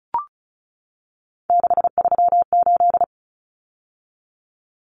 Morse code, heard as a keyed tone, about 34 wpm, 700 Hz. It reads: 638
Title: Numbers - 3-Digit 35wpm